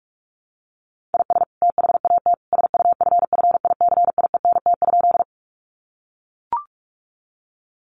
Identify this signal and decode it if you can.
Morse code, heard as a keyed tone, about 44 wpm, 700 Hz. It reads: IS THAT SUFFICIENT?
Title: Sentences from Top 1400 Words 45wpm